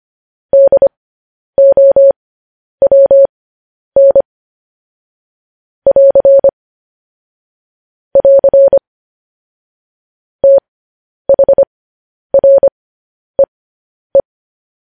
Morse code, heard as a keyed tone, about 25 wpm, spaced as for 10 wpm, 550 Hz. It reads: DOWN <AR> <AR> THREE
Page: news headlines in morse code at 10 wpm